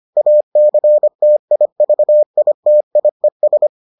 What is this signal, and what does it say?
Morse code, heard as a keyed tone, about 25 wpm, 600 Hz. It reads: ACTIVITIES